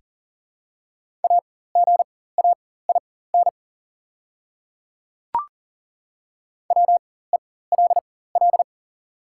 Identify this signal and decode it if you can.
Morse code, heard as a keyed tone, about 40 wpm, spaced as for 19 wpm, 700 Hz. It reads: AGAIN WELL
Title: 4x Character Spacing - Top 200 Words - 40wpm